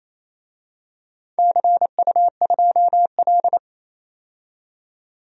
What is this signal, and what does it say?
Morse code, heard as a keyed tone, about 27 wpm, 700 Hz. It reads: CU2L
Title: DXCC-100 Call Signs - 4 Characters - 28wpm